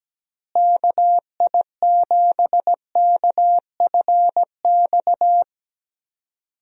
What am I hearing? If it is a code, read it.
Morse code, heard as a keyed tone, about 17 wpm, 700 Hz. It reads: KI7KFX
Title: USA Call Signs - 6 Characters - Repeated 3x - 17wpm